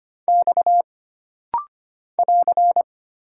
Morse code, heard as a keyed tone, about 25 wpm, 700 Hz. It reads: X <AR>
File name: Single Character (Letter:Number:Punctuation:Prosign) - Rapid-Fire - 25wpm